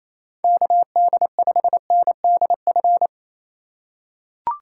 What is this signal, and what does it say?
Morse code, heard as a keyed tone, about 27 wpm, 700 Hz. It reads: KD5NDF
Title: USA Call Signs - 6 Characters - 4x Character Spacing - 28wpm